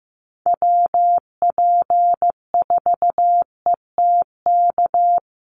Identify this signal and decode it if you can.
Morse code, heard as a keyed tone, about 15 wpm, 700 Hz. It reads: WP4ETK
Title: USA Call Signs Repeated 2x 15wpm